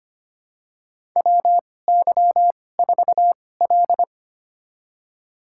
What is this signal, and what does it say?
Morse code, heard as a keyed tone, about 25 wpm, spaced as for 18 wpm, 700 Hz. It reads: WY4L
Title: USA Call Signs - 2x Character Spacing - 25wpm